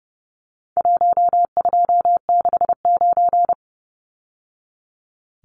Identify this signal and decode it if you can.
Morse code, heard as a keyed tone, about 30 wpm, 700 Hz. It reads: 1269
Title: Numbers - Contest Style - 4-Digit 30wpm